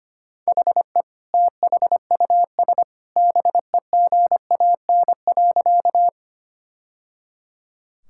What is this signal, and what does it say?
Morse code, heard as a keyed tone, about 25 wpm, 700 Hz. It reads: HE THUS BEGAN.
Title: Sentences from Top 1000 Words - Review Part 5 - 25wpm